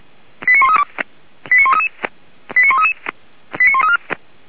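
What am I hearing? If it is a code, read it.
Five-tone selective call (ZVEI), heard as quick runs of five tones, tones of about 80 ms. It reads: 89132 89133 89133 89134
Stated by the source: Five-tone ZVEI call received by SDR, from the feuerwehr-orlamuende.org fire brigade audio server.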